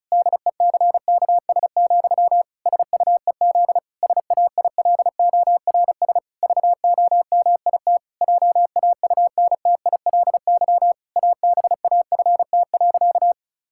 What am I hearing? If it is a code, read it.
Morse code, heard as a keyed tone, about 36 wpm, 700 Hz. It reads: DECKS, SUEZ SAILORS VOMIT JAUNTILY ABAFT.
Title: Pangram Challenge v2 - 35wpm